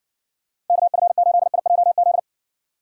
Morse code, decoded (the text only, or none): KF8IJZ